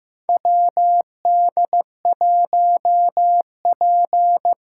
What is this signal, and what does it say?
Morse code, heard as a keyed tone, about 15 wpm, 700 Hz. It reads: WD1P
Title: USA Callsign DE Callsign - 4 Characters - 15wpm